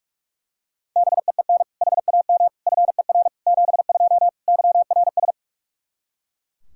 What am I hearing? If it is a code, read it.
Morse code, heard as a keyed tone, about 41 wpm, 700 Hz. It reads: BEEN HAM FER 72 YRS
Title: Practice QSO - Exchange 3 - More Info - 45wpm